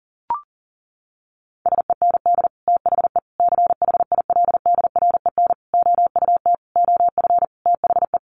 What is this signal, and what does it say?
Morse code, heard as a keyed tone, about 37 wpm, 700 Hz. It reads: SEND THE CHILDREN OUT OF THE
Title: Sentences from Top 1500 Words 40wpm